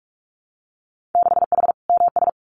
Morse code, heard as a keyed tone, about 44 wpm, 700 Hz. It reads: 6H MS
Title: Winter Field Day Exchanges Encore 45wpm